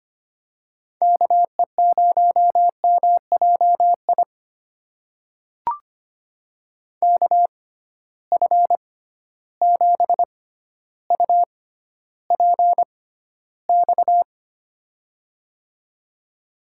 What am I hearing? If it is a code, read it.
Morse code, heard as a keyed tone, about 25 wpm, spaced as for 13 wpm, 700 Hz. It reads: KE0MJI KF7UPX
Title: USA Call Signs - 6 Characters - 6x Character Spacing - 25wpm